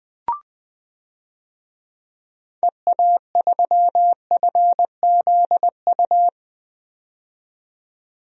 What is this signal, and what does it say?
Morse code, heard as a keyed tone, about 20 wpm, 700 Hz. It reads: EA3FZU
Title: DXCC-100 Call Signs - 6 Characters - 20wpm